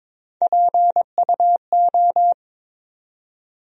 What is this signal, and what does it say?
Morse code, heard as a keyed tone, about 23 wpm, 700 Hz. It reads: PUO
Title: USA Call Sign Postfixes - 3 Letters 22wpm